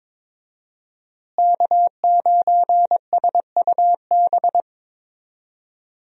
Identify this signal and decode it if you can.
Morse code, heard as a keyed tone, about 22 wpm, 700 Hz. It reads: K9SUB
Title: USA Call Signs - 5 Characters - Repeated 2x - 22wpm